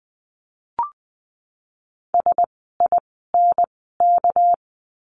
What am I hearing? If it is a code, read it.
Morse code, heard as a keyed tone, about 20 wpm, spaced as for 15 wpm, 700 Hz. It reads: SINK
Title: 199 Lesson 32f - Letter k - Words 20wpm